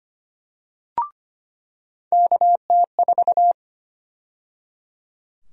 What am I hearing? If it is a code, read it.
Morse code, heard as a keyed tone, about 25 wpm, 700 Hz. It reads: KT4
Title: USA Call Sign Prefixes - 25wpm